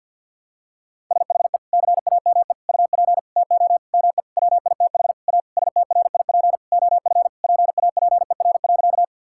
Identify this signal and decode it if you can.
Morse code, heard as a keyed tone, about 50 wpm, 700 Hz. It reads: SHE CAME UP TO ME WITH A STRIP OF PAPER.
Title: Sentences from Top 1600 Words 50wpm